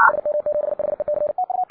Morse code, heard as a keyed tone, about 58 wpm, 590 Hz. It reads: SM7HEC